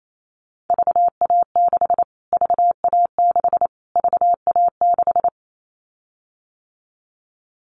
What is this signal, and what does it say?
Morse code, heard as a keyed tone, about 28 wpm, 700 Hz. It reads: VA6 VA6 VA6
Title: Callsign Prefixes for Top 25 DXCC - Repeated 3x - 28wpm